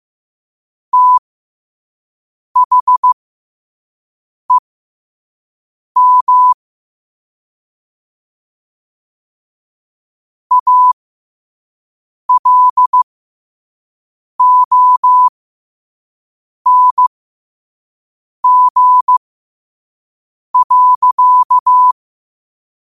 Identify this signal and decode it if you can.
Morse code, heard as a keyed tone, about 15 wpm, spaced as for 5 wpm, 1000 Hz. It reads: THEM ALONG.
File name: qotdfiles / 5wpm